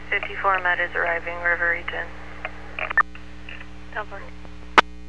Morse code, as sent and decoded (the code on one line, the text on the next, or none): none